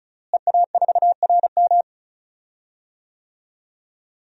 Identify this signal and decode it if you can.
Morse code, heard as a keyed tone, about 34 wpm, 700 Hz. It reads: EA4RM
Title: DXCC-100 Call Signs - 5 Characters - Repeated 3x - 35wpm